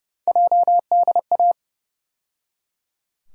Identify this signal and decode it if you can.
Morse code, heard as a keyed tone, about 32 wpm, 700 Hz. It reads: JDA
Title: USA Call Sign Postfixes - 3 Letters 30wpm